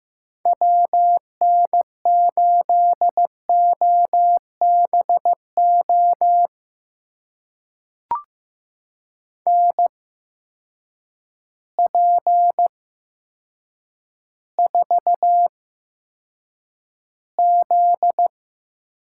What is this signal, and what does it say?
Morse code, heard as a keyed tone, about 15 wpm, spaced as for 8 wpm, 700 Hz. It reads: WN8OBO NP4Z
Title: USA Call Signs - 8x Character Spacing - 15wpm